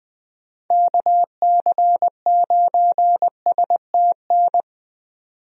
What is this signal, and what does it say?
Morse code, heard as a keyed tone, about 20 wpm, 700 Hz. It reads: KC9STN
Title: USA Call Signs - 6 Characters - 20wpm